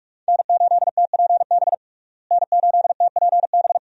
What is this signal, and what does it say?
Morse code, heard as a keyed tone, about 47 wpm, 700 Hz. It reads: N8TPB N8TPB
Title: USA Call Signs - 5 Characters - Repeated 3x - 45wpm